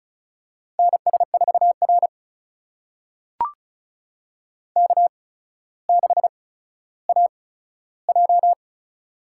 Morse code, heard as a keyed tone, about 36 wpm, spaced as for 15 wpm, 700 Hz. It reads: NS4R K6AJ
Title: USA Call Signs - 4 Characters - 8x Character Spacing - 35wpm